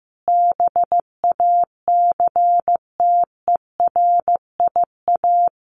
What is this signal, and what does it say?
Morse code, heard as a keyed tone, about 15 wpm, 700 Hz. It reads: BACTERIA
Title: A Sound Of Thunder 15wpm